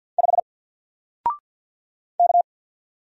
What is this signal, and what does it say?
Morse code, heard as a keyed tone, about 48 wpm, 700 Hz. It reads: 5 K
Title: Single Character (Letter:Number:Punctuation:Prosign) Alt 50wpm